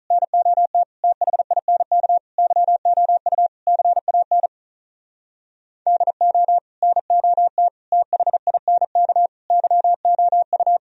Morse code, heard as a keyed tone, about 43 wpm, 700 Hz. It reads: NOT THINK YOU CAN DO NOT THINK YOU
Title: Speed-Racing - Sets of 5 Words 35wpm